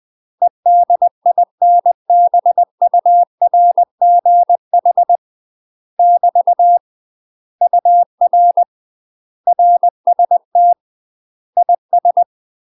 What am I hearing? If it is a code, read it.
Morse code, heard as a keyed tone, about 20 wpm, 700 Hz. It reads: EDINBURGH <BT> UR RST IS
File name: UK MASTER QSOs TRAINING / 20WPM / UK MASTER QSOs 0046